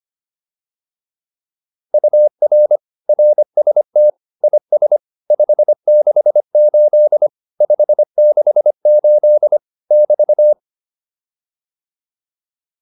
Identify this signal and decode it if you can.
Morse code, heard as a keyed tone, about 25 wpm, 600 Hz. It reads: UR RST IS 568 568 <BT>